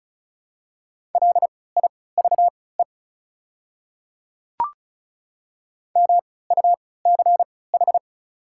Morse code, heard as a keyed tone, about 34 wpm, spaced as for 20 wpm, 700 Hz. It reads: LIVE MUCH
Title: 3x Character Spacing - Top 200 Words - 35wpm